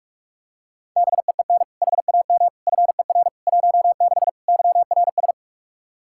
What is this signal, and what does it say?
Morse code, heard as a keyed tone, about 45 wpm, 700 Hz. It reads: BEEN HAM FER 16 YRS
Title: Practice QSO - Exchange 3 - More Info - 45wpm